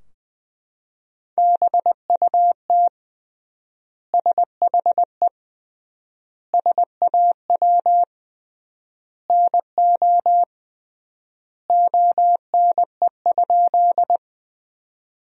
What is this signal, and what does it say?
Morse code, heard as a keyed tone, about 21 wpm, 700 Hz. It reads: BUT SHE SAW NO ONE?
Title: 3x Word-Spacing - Sentences from Top 300 Words - 20wpm